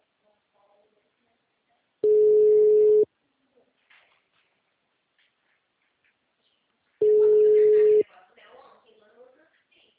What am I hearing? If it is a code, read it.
Ringback tone (Europe, 425 Hz), with speech.